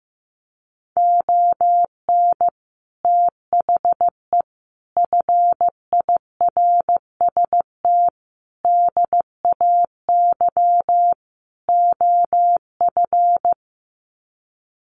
Morse code, heard as a keyed tone, about 15 wpm, 700 Hz. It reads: ON THE FIRST DAY OF
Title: Sets of 5 Words Encore 15wpm